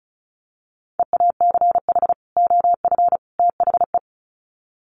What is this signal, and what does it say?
Morse code, heard as a keyed tone, about 34 wpm, 700 Hz. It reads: EACH OF THE
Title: Sets of 3 Words 35wpm